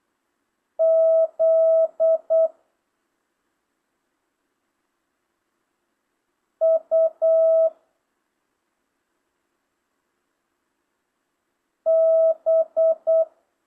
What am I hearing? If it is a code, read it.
Morse code, heard as a keyed tone, about 8 wpm, 650 Hz. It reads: Z U B
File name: jh8cbh / 2025cw